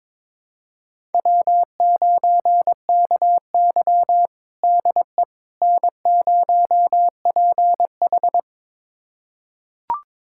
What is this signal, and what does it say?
Morse code, heard as a keyed tone, about 22 wpm, 700 Hz. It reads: W9KY DE N0PH
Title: USA Callsign DE Callsign - 4 Characters - 22wpm